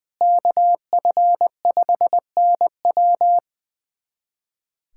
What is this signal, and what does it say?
Morse code, heard as a keyed tone, about 20 wpm, 700 Hz. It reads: KF5NW
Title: USA Callsign DE Callsign - 5 Characters - 20wpm